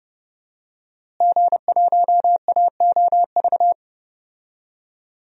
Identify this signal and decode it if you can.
Morse code, heard as a keyed tone, about 30 wpm, 700 Hz. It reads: G1AOV
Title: DXCC-100 Call Signs - 5 Characters - Repeated 3x - 30wpm